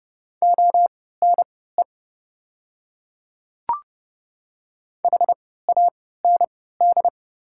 Morse code, heard as a keyed tone, about 30 wpm, spaced as for 17 wpm, 690 Hz. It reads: ONE HAND
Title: 3x Character Spacing - Top 500 Words - 30wpm